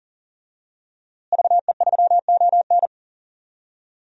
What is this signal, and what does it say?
Morse code, heard as a keyed tone, about 39 wpm, 700 Hz. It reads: VE3ON
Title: DXCC-100 Call Signs - 5 Characters - Repeated 2x - 40wpm